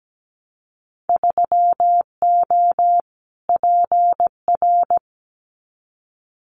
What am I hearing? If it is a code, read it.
Morse code, heard as a keyed tone, about 17 wpm, 700 Hz. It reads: 3O PR
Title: Winter Field Day Exchanges Encore 17wpm